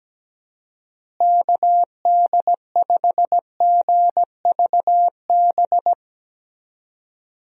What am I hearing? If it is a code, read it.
Morse code, heard as a keyed tone, about 17 wpm, 700 Hz. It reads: KD5GVB
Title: USA Call Signs - 6 Characters - Repeated 2x - 17wpm